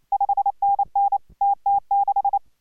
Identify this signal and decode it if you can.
Morse code, heard as a keyed tone, about 28 wpm, 810 Hz. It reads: 5NN TT6